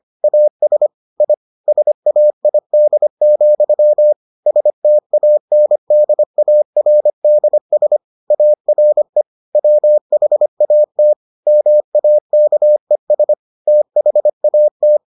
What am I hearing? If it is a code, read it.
Morse code, heard as a keyed tone, about 25 wpm, 600 Hz. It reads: AS I SAID, STANDARDS ARE WHAT MAKES THAT